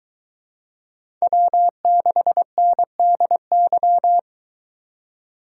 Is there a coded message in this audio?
Yes: Morse code from 1.2 to 4.2 s, heard as a keyed tone, about 23 wpm, 700 Hz. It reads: W6NDY